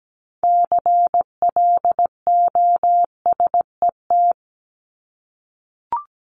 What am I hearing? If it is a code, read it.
Morse code, heard as a keyed tone, about 17 wpm, 700 Hz. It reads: CLOSET
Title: Top 1751-2000 Words 17wpm